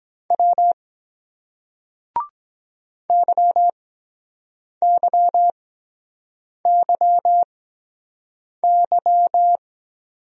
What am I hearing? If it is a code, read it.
Morse code, heard as a keyed tone, about 23 wpm, spaced as for 7 wpm, 700 Hz. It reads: W YYYY